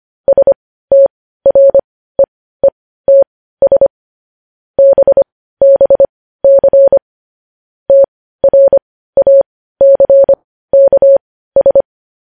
Morse code, heard as a keyed tone, about 25 wpm, spaced as for 15 wpm, 550 Hz. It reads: STREETS BBC TRACKS
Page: news headlines in morse code at 15 wpm